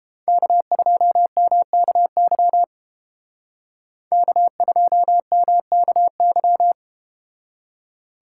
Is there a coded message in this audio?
Yes: Morse code from 0.3 to 6.7 s, heard as a keyed tone, about 32 wpm, 700 Hz. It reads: K2MKY K2MKY